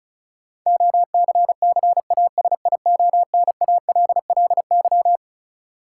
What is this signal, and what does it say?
Morse code, heard as a keyed tone, about 34 wpm, 700 Hz. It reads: OCCASIONALLY
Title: Sentences from Top 1300 Words - New Word Drill Alt - 35wpm